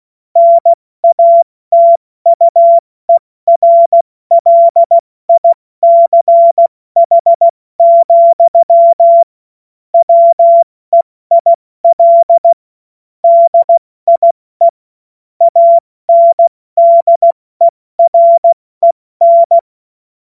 Morse code, heard as a keyed tone, about 16 wpm, 680 Hz. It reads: NATUERLICH, WEIL DIE ANDEREN
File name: Morsekurs / Witze / Witz 18. April 70 F80